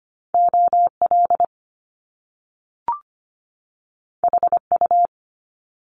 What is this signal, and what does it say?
Morse code, heard as a keyed tone, about 25 wpm, 700 Hz. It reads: OL HU
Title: Top Two Letter Combinations in Words 25wpm